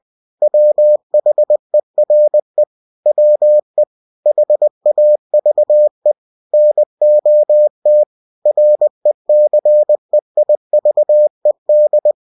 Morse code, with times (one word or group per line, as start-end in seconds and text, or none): 0.4-2.6 WHERE
3.1-3.8 WE
4.3-6.1 HAVE
6.5-8.0 NOT
8.5-12.1 RECEIVED